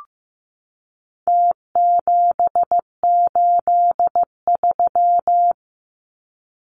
Morse code, heard as a keyed tone, about 15 wpm, 700 Hz. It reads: T783
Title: Numbers - Contest Style - 4-Digit 15wpm